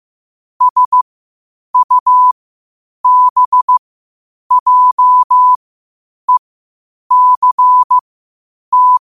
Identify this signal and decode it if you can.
Morse code, heard as a keyed tone, about 15 wpm, spaced as for 8 wpm, 1000 Hz. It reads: SUBJECT
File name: qotdfiles / 8wpm